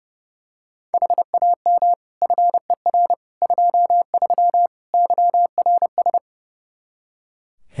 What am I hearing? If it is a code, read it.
Morse code, heard as a keyed tone, about 30 wpm, 700 Hz. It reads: HAM FER 23 YRS
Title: Practice QSO - Exchange 3 - More Info - 30wpm